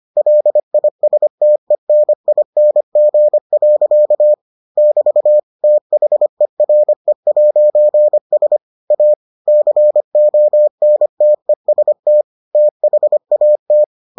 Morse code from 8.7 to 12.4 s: A CONTEST

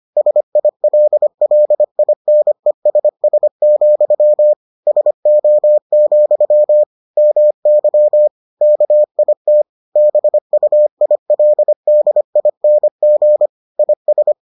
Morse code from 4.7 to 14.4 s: SO, MY KIT BUILDING IS

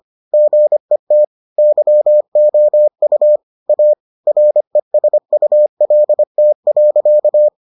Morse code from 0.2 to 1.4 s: GET

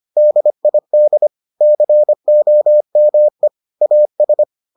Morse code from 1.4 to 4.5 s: COME AS

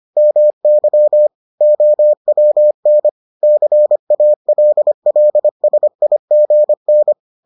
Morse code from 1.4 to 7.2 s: OWN CALLSIGN

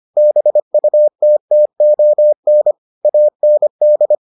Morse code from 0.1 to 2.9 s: BUTTON

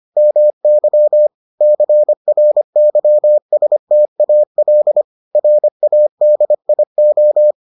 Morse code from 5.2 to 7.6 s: RADIO

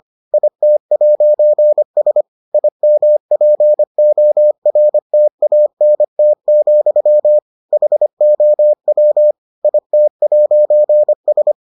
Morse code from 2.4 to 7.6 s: IMPORTANT,

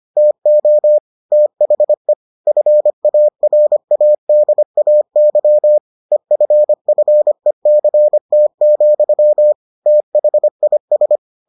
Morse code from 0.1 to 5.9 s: TO THE FARADAY

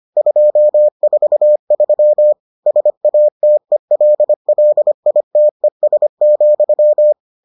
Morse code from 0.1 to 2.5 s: 243

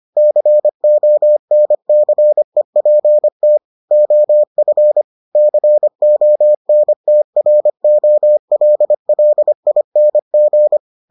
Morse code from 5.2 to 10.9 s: CONTROLLING

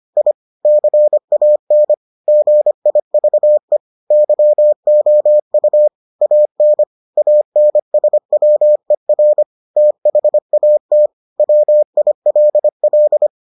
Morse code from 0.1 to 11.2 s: I CAN GIVE YOU AN ANSWER THAT

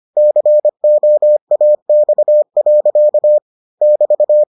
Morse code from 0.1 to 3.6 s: COAX.

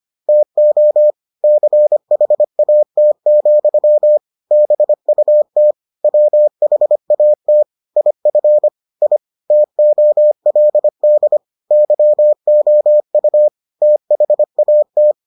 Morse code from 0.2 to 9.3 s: TO CHAT, BUT WHAT IF I

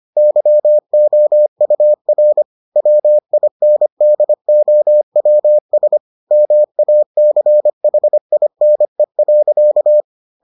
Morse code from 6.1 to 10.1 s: MACHINE.

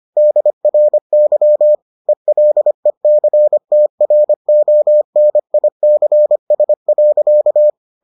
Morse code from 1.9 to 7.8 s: ELECTRONICS.

